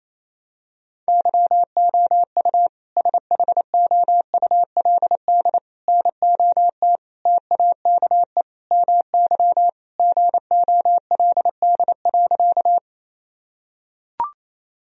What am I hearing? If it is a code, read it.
Morse code, heard as a keyed tone, about 28 wpm, 700 Hz. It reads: YOU SHOULD NOT TAKE MY GOLD.
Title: Sentences from Top 1000 Words - Review Part 3 - 28wpm